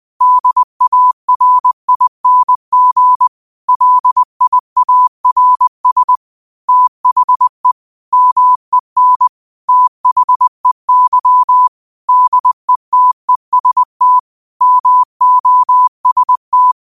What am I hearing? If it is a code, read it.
Morse code, heard as a keyed tone, about 20 wpm, 1000 Hz. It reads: DARING LIARS THE MEN THEY DETEST MOST
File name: 20wpm